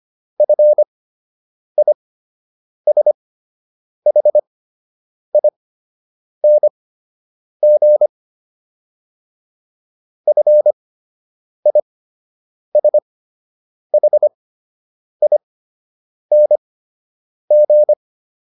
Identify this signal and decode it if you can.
Morse code, heard as a keyed tone, about 25 wpm, spaced as for 8 wpm, 600 Hz. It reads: FISHING FISHING